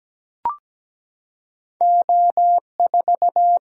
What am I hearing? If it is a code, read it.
Morse code, heard as a keyed tone, about 17 wpm, 700 Hz. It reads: O4
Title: Single-Letter Single-Number 17wpm